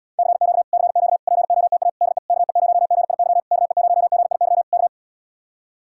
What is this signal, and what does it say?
Morse code, heard as a keyed tone, about 77 wpm, 700 Hz. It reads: CQ CQ POTA DE KE0GIY KE0GIY K